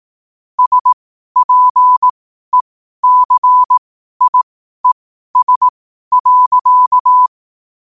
Morse code, heard as a keyed tone, about 18 wpm, spaced as for 12 wpm, 1000 Hz. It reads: SPECIES.
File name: qotdfiles / 12wpm